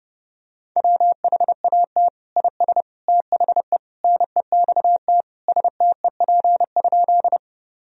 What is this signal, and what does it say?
Morse code, heard as a keyed tone, about 32 wpm, 700 Hz. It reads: WHAT IS THE NEXT STEP?
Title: Sentences from Top 1000 Words - Review Part 2 - 30wpm